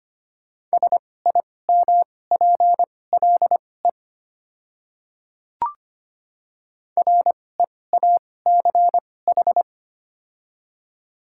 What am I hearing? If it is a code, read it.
Morse code, heard as a keyed tone, about 25 wpm, spaced as for 18 wpm, 700 Hz. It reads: SIMPLE REACH
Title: 2x Character Spacing - Top 500 Words - 25wpm